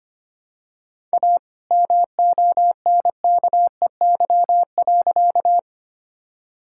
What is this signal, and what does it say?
Morse code, heard as a keyed tone, about 25 wpm, 700 Hz. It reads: A MONKEY.
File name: practice / 25wpm / All-Age Appropriate Jokes - 25wpm